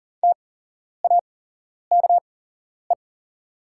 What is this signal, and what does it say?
Morse code, heard as a keyed tone, about 40 wpm, spaced as for 11 wpm, 700 Hz. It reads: TAKE
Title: 8x Character Spacing - Top 300 Words - 40wpm